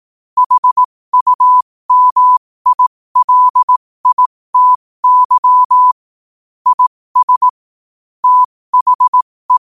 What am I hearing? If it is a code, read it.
Morse code, heard as a keyed tone, about 18 wpm, spaced as for 15 wpm, 1000 Hz. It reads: HUMILITY IS THE
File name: qotdfiles / 15wpm